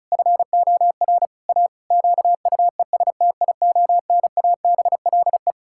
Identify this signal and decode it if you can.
Morse code, heard as a keyed tone, about 36 wpm, 700 Hz. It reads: FOR A QUESTIONABLE
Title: Pangram Challenge v2 - 35wpm